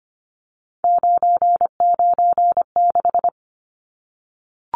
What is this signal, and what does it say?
Morse code, heard as a keyed tone, about 25 wpm, 700 Hz. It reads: 996
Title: Numbers - 3-Digit 25wpm